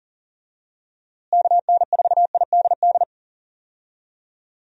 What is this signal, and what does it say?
Morse code, heard as a keyed tone, about 40 wpm, 700 Hz. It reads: KN4IDD
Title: USA Call Signs - 6 Characters - Repeated 3x - 40wpm